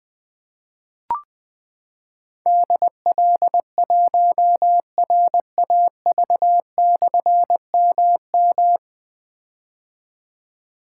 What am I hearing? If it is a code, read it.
Morse code, heard as a keyed tone, about 20 wpm, 700 Hz. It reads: DL1RAV/MM
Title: Difficult Call Signs 20wpm